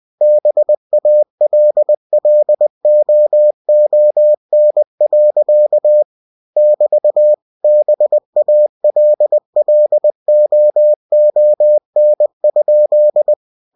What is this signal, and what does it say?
Morse code, heard as a keyed tone, about 20 wpm, 600 Hz. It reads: BALLOON. <BT> BALLOON?